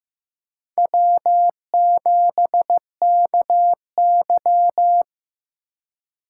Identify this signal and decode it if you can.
Morse code, heard as a keyed tone, about 18 wpm, 700 Hz. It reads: W7KY
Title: Speed Racing - USA 1x2 Call Signs 15wpm